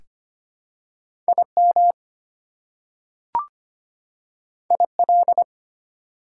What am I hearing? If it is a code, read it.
Morse code, heard as a keyed tone, about 25 wpm, 700 Hz. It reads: IM IL
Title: Prefixes and Suffixes 25wpm